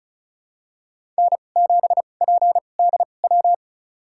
Morse code, heard as a keyed tone, about 36 wpm, spaced as for 25 wpm, 700 Hz. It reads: N7PDW
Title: USA Call Signs - 5 Characters - 2x Character Spacing - 35wpm